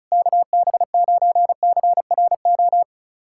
Morse code, heard as a keyed tone, about 36 wpm, 700 Hz. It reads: KB9CRO
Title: USA Callsign DE Callsign - 6 Characters - 35wpm